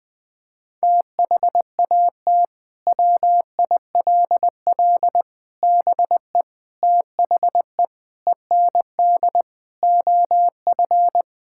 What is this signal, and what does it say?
Morse code, heard as a keyed tone, about 20 wpm, 700 Hz. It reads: THAT WILL BE THE END OF
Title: Sentences from Top 1000 Words - Review Part 6 - 20wpm